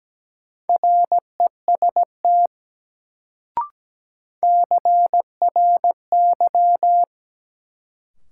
Morse code, heard as a keyed tone, about 17 wpm, 700 Hz. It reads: REST CRY
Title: Sentences from Top 500 Words - New Word Drill - 17wpm